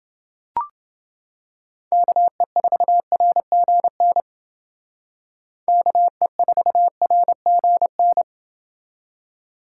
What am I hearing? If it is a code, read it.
Morse code, heard as a keyed tone, about 28 wpm, 700 Hz. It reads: KE4RGN KE4RGN